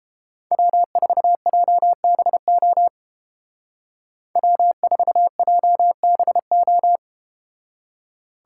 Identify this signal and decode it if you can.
Morse code, heard as a keyed tone, about 30 wpm, 700 Hz. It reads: W4JBO W4JBO